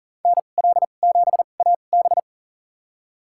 Morse code, heard as a keyed tone, about 40 wpm, spaced as for 29 wpm, 700 Hz. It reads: NL7AB
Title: USA Call Signs - 5 Characters - 2x Character Spacing - 40wpm